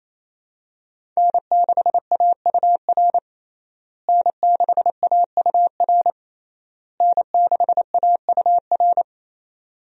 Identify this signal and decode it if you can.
Morse code, heard as a keyed tone, about 28 wpm, 700 Hz. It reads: N6AUR N6AUR N6AUR
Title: USA Call Signs - 5 Characters - Repeated 3x - 28wpm